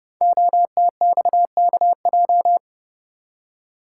Morse code, heard as a keyed tone, about 29 wpm, 700 Hz. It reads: OTXKJ
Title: Code Groups - 5 Letter:Numbers - 30wpm